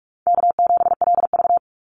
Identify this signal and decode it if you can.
Morse code, heard as a keyed tone, about 44 wpm, 700 Hz. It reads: K7LV
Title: Easy Call Signs Repeated 2x - 45wpm